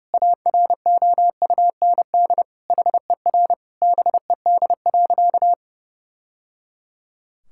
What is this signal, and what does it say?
Morse code, heard as a keyed tone, about 30 wpm, 700 Hz. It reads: AROUND HER BED.
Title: Sentences from Top 1000 Words - Review Part 6 - 30wpm